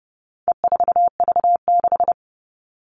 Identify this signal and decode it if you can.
Morse code, heard as a keyed tone, about 30 wpm, 700 Hz. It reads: E4V6
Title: All Cut Numbers - 4-Digit 30wpm